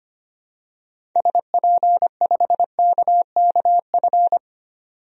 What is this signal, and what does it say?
Morse code, heard as a keyed tone, about 25 wpm, 700 Hz. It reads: SP5KKF
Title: DXCC-100 Call Signs - 6 Characters - 25wpm